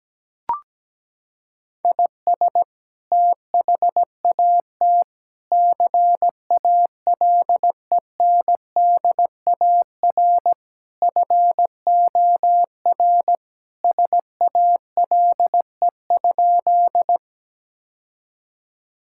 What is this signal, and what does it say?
Morse code, heard as a keyed tone, about 17 wpm, 700 Hz. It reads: IS THAT CALENDAR FOR SALE?
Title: Sentences from Top 1600 Words 17wpm